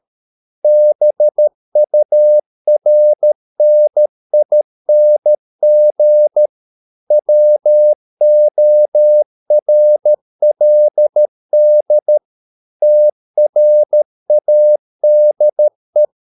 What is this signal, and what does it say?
Morse code, heard as a keyed tone, about 13 wpm, 600 Hz. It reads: BURNING WORLD TRADE